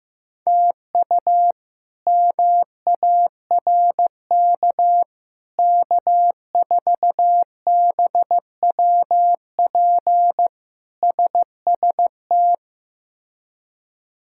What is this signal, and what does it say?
Morse code, heard as a keyed tone, about 16 wpm, 700 Hz. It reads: TU MARK K4BWP SST
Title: K1USN SST QSO Practice - 15wpm